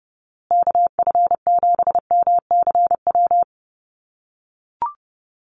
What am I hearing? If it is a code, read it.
Morse code, heard as a keyed tone, about 30 wpm, 700 Hz. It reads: KF7MCW
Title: USA Call Signs Repeated 2x 30wpm